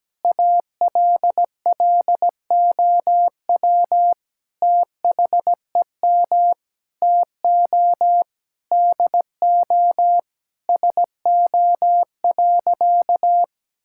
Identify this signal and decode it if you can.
Morse code, heard as a keyed tone, about 17 wpm, 700 Hz. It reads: ALLOW THEM TO DO SO.
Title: Sentences from Top 1000 Words - Review Part 5 - 17wpm